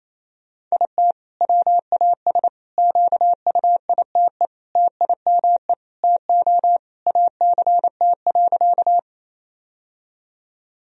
Morse code, heard as a keyed tone, about 28 wpm, 700 Hz. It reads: IT WAS QUITE TIME TO ACT.
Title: Sentences from Top 1000 Words - Review Part 1 - 28wpm